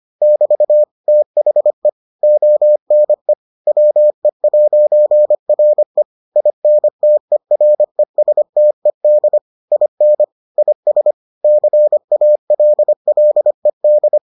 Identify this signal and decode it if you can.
Morse code, heard as a keyed tone, about 25 wpm, 600 Hz. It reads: <BT> THE ONE WE'RE INTERESTED IN IS CALLED